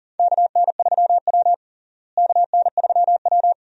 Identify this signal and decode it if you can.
Morse code, heard as a keyed tone, about 37 wpm, 700 Hz. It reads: KN3W KN3W
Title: USA Call Signs - 4 Characters - Repeated 2x - 40wpm